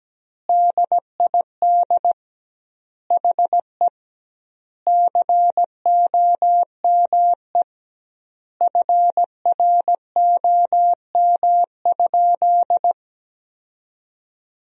Morse code, heard as a keyed tone, about 17 wpm, 700 Hz. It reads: DID HE COME FROM?
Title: 2x Word-Spacing - Sentences from Top 200 Words - 17wpm